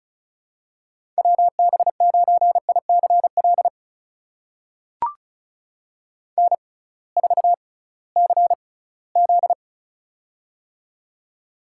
Morse code, heard as a keyed tone, about 36 wpm, spaced as for 20 wpm, 700 Hz. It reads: WB9ICL N4CZ
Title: USA Call Signs - 6x Character Spacing - 35wpm